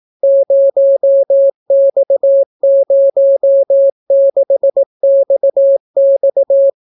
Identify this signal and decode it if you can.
Morse code, heard as a keyed tone, about 18 wpm, 550 Hz. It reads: 0X06XX